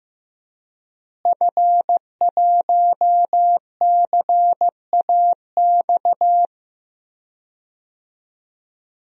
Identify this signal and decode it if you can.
Morse code, heard as a keyed tone, about 15 wpm, 700 Hz. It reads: F1CAX
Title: DXCC-100 Call Signs - 5 Characters - Repeated 2x - 15wpm